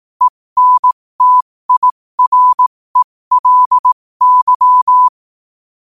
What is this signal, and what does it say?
Morse code, heard as a keyed tone, about 18 wpm, spaced as for 15 wpm, 1000 Hz. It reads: ENTIRELY
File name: qotdfiles / 15wpm